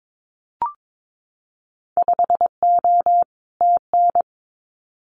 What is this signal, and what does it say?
Morse code, heard as a keyed tone, about 22 wpm, 700 Hz. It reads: 5O TN
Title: Winter Field Day Exchanges Encore 22wpm